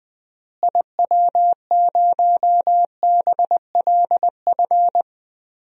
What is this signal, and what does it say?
Morse code, heard as a keyed tone, about 21 wpm, 700 Hz. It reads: IW0BLF
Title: DXCC-100 Call Signs - 6 Characters - Repeated 3x - 20wpm